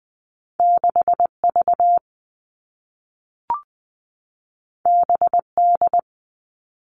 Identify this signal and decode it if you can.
Morse code, heard as a keyed tone, about 20 wpm, 700 Hz. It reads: 6V BD
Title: All Cut Numbers - 2-Digit 20wpm